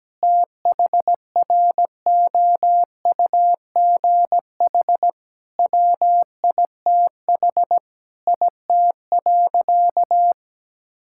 Morse code, heard as a keyed tone, about 17 wpm, 700 Hz. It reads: THROUGH WITH IT.
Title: Sentences from Top 1000 Words - Review Part 6 - 17wpm